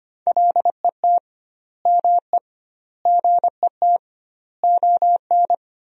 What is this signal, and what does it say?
Morse code, heard as a keyed tone, about 25 wpm, 700 Hz. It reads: LET ME GET ON
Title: 2x Word-Spacing - Sentences from Top 300 Words - 25wpm